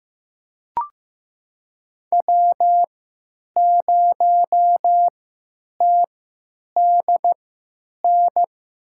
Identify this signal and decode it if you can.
Morse code, heard as a keyed tone, about 15 wpm, spaced as for 9 wpm, 700 Hz. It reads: W0TDN
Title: USA Call Signs - 5 Characters - 3x Character Spacing - 15wpm